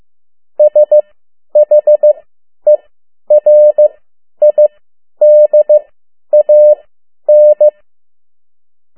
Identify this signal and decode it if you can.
Morse code, heard as a keyed tone, about 15 wpm, spaced as for 10 wpm, 600 Hz. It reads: SHERIDAN
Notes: Around the World in 80 Days paragraph one Morse code 10 WPM